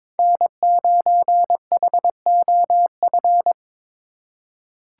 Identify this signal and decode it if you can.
Morse code, heard as a keyed tone, about 22 wpm, 700 Hz. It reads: N9HOF
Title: USA Call Signs - 2x Character Spacing - 22wpm